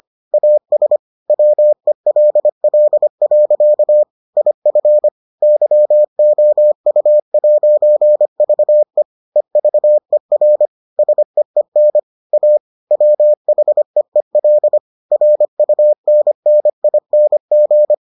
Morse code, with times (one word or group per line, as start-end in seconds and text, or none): 0.3-1.0 AS
1.3-4.0 WELL.
4.4-5.1 IF
5.4-9.0 YOU'VE
9.4-10.7 EVER
11.0-12.0 SEEN
12.3-12.6 A
12.9-14.8 WHEEL
15.1-17.9 RUNNING